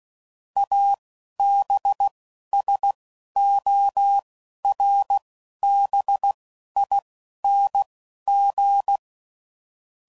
Morse code, heard as a keyed tone, about 16 wpm, spaced as for 11 wpm, 800 Hz. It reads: ABSORBING
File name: CW Code Groups @10WPM / Technical Subjects / XC